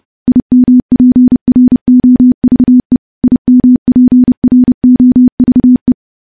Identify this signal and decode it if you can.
Morse code, heard as a keyed tone, about 30 wpm, 250 Hz. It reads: IMPROVE IMPROVE
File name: morse / morse-training-30WPM